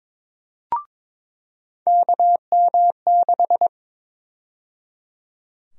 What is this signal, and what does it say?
Morse code, heard as a keyed tone, about 22 wpm, 700 Hz. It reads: KM6
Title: USA Call Sign Prefixes - 22wpm